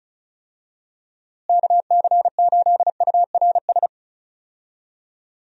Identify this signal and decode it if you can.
Morse code, heard as a keyed tone, about 36 wpm, 700 Hz. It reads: KC8URS
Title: USA Call Signs - 6 Characters - Repeated 3x - 35wpm